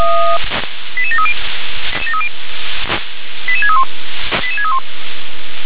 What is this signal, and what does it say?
Five-tone selective call (ZVEI), heard as quick runs of five tones, tones of about 70 ms. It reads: 88730 88631 88631